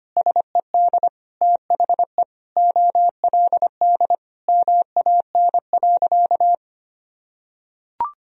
Morse code, heard as a keyed tone, about 25 wpm, 700 Hz. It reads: SED THE OLD MAN.
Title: Sentences from Top 200 Words and Abbreviations 25wpm